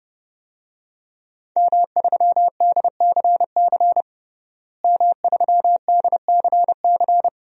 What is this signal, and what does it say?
Morse code, heard as a keyed tone, about 32 wpm, 700 Hz. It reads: M3DCC M3DCC
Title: DXCC-100 Call Signs - 5 Characters - Repeated 3x - 30wpm